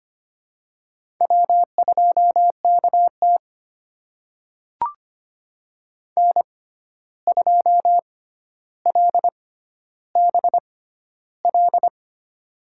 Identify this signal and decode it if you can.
Morse code, heard as a keyed tone, about 25 wpm, spaced as for 12 wpm, 700 Hz. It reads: W2KT N2LBL
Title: USA Call Signs - 6x Character Spacing - 25wpm